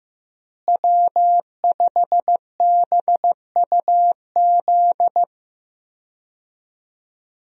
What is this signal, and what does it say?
Morse code, heard as a keyed tone, about 15 wpm, 700 Hz. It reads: W5BUZ
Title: USA Call Signs - 5 Characters - Repeated 2x - 15wpm